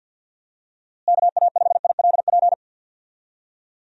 Morse code, heard as a keyed tone, about 47 wpm, 700 Hz. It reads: KA5ILP
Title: USA Call Signs - 6 Characters - Repeated 2x - 50wpm